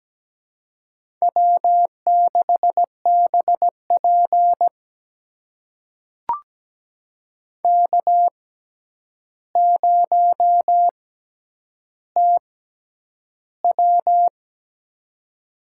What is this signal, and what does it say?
Morse code, heard as a keyed tone, about 17 wpm, spaced as for 9 wpm, 700 Hz. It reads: W6BP K0TW
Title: USA Call Signs - 4 Characters - 6x Character Spacing - 17wpm